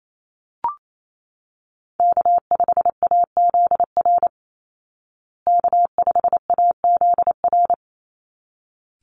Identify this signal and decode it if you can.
Morse code, heard as a keyed tone, about 28 wpm, 700 Hz. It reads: K5AZR K5AZR
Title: Call Signs Repeated 3x - 28wpm